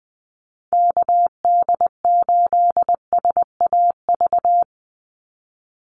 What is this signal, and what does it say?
Morse code, heard as a keyed tone, about 20 wpm, 700 Hz. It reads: KD8SAV
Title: Call Signs - 20wpm